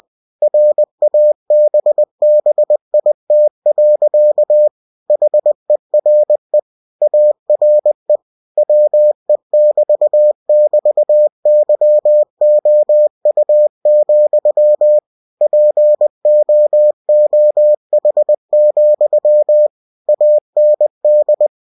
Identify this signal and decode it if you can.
Morse code, heard as a keyed tone, about 20 wpm, 600 Hz. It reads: RABBIT. HERE ARE WE <BT> <BT> YOU, POOH, AND